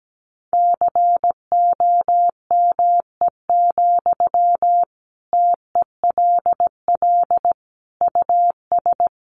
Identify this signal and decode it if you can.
Morse code, heard as a keyed tone, about 17 wpm, 700 Hz. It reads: COME, TELL US
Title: Sentences from Top 800 Words 17wpm